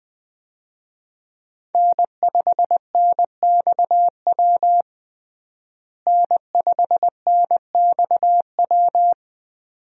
Morse code, heard as a keyed tone, about 20 wpm, 700 Hz. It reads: N5NXW N5NXW
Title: USA Call Signs - 5 Characters - Repeated 2x - 20wpm